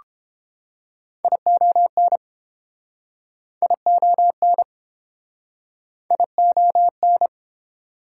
Morse code, heard as a keyed tone, about 33 wpm, 700 Hz. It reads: ION ION ION